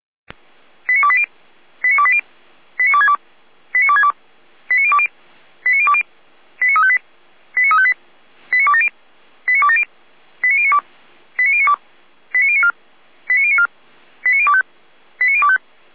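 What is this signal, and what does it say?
Five-tone selective call (ZVEI), heard as quick runs of five tones, tones of about 70 ms. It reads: 89280 89280 89262 89262 89020 89020 89358 89358 89270 89270 89092 89092 89094 89094 89025 89025
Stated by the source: ZVEI five-tone selective call received by SDR.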